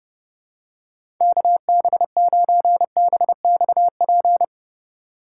Morse code, heard as a keyed tone, about 30 wpm, 700 Hz. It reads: KB9BXP
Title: USA Call Signs - 6 Characters - 2x Character Spacing - 30wpm